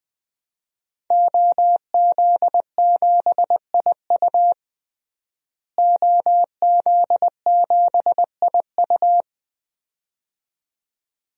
Morse code, heard as a keyed tone, about 20 wpm, 700 Hz. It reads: OZ7IU OZ7IU
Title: DXCC-100 Call Signs - 5 Characters - Repeated 2x - 20wpm